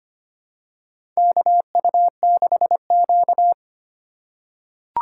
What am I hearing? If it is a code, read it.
Morse code, heard as a keyed tone, about 25 wpm, 700 Hz. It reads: KU6Q
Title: USA Call Signs - 4 Characters - 2x Character Spacing - 25wpm